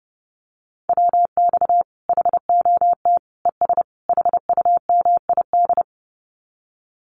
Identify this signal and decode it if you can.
Morse code, heard as a keyed tone, about 30 wpm, 700 Hz. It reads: WX HOT ES HUMID
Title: Common QSO Phrases 30wpm